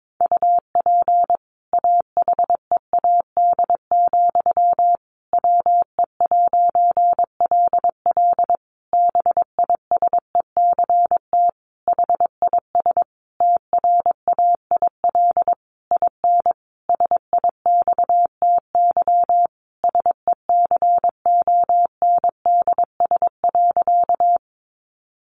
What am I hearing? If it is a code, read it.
Morse code, heard as a keyed tone, about 22 wpm, 700 Hz. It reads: UP AHEAD, WE'LL BISECT HIS TRAIL IN SIXTY SECONDS.
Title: A Sound Of Thunder 22wpm